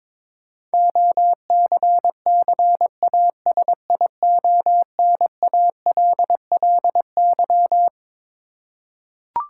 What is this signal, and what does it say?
Morse code, heard as a keyed tone, about 22 wpm, 700 Hz. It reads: OCCASIONALLY
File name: Twelve Letter Words - 22wpm